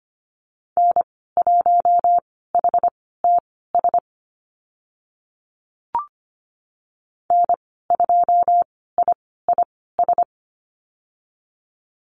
Morse code, heard as a keyed tone, about 25 wpm, spaced as for 16 wpm, 700 Hz. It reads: N1HTS N2IIS
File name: Lesson 15g - Number 5 - Call Signs 25wpm